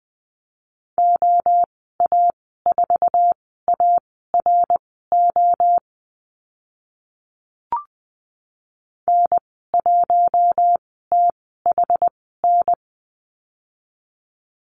Morse code, heard as a keyed tone, about 20 wpm, spaced as for 15 wpm, 700 Hz. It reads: OA4ARO N1THN
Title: 082 Lesson 14g - Number 2 - Call Signs 20wpm